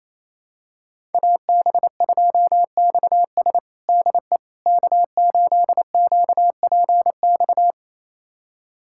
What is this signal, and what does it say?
Morse code, heard as a keyed tone, about 28 wpm, 700 Hz. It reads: AB2XS DE K8QPX
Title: USA Callsign DE Callsign - 5 Characters - 28wpm